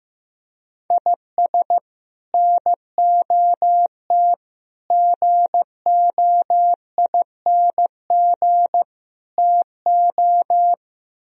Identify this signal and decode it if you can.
Morse code, heard as a keyed tone, about 15 wpm, 700 Hz. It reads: IS NOT GOING TO
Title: Speed-Racing - Sets of 4 Words 15wpm